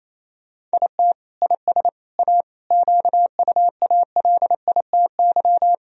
Morse code, heard as a keyed tone, about 27 wpm, 700 Hz. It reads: IT IS A QUALITY
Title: Sentences from Top 1000 Words - Review Part 4 - 28wpm